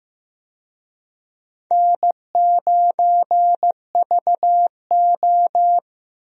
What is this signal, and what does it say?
Morse code, heard as a keyed tone, about 15 wpm, 700 Hz. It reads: N9VO
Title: USA Call Signs - 4 Characters - Repeated 2x - 15wpm